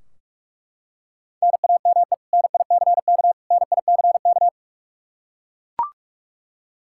Morse code, heard as a keyed tone, about 41 wpm, 700 Hz. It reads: NAME NICK NICK
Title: Practice QSO - Exchange 1 - Beginning - 45wpm